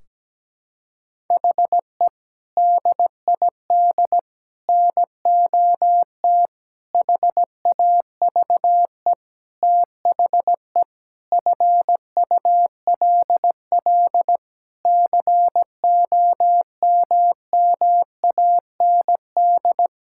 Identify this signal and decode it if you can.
Morse code, heard as a keyed tone, about 17 wpm, 700 Hz. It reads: HE DID NOT HAVE THE FULL COMMAND
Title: Sentences from Top 1600 Words 17wpm